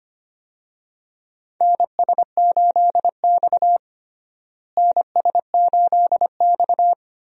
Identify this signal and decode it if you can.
Morse code, heard as a keyed tone, about 25 wpm, 700 Hz. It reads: NS8X NS8X
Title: USA Call Signs - 4 Characters - Repeated 2x - 25wpm